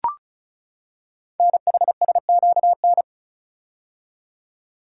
Morse code, heard as a keyed tone, about 36 wpm, 700 Hz. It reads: NHSQN
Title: Code Groups - 5 Letter:Numbers - 35wpm